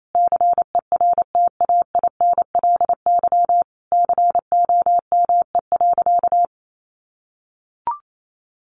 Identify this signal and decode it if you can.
Morse code, heard as a keyed tone, about 28 wpm, 700 Hz. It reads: CERTAINLY COME.
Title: Sentences from Top 900 Words 28wpm